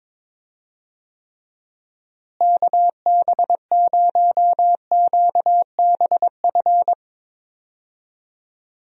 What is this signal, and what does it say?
Morse code, heard as a keyed tone, about 22 wpm, 700 Hz. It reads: KB0QBF
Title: USA Call Signs - 6 Characters - Repeated 2x - 22wpm